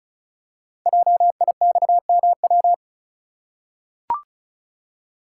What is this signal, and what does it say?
Morse code, heard as a keyed tone, about 36 wpm, 700 Hz. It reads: JIXMW
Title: Code Groups - 5 Letters - 35wpm